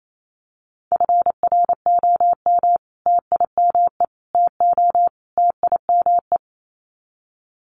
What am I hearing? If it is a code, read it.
Morse code, heard as a keyed tone, about 28 wpm, 700 Hz. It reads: FROM TIME TO TIME
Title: Sets of 4 Words 28wpm